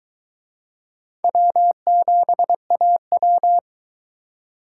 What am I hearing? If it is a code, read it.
Morse code, heard as a keyed tone, about 21 wpm, 700 Hz. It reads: W7AW